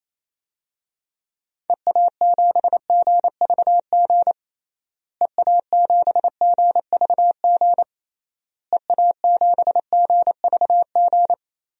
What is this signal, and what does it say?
Morse code, heard as a keyed tone, about 28 wpm, 700 Hz. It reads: EA7GVG EA7GVG EA7GVG
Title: DXCC-100 Call Signs - 6 Characters - Repeated 3x - 28wpm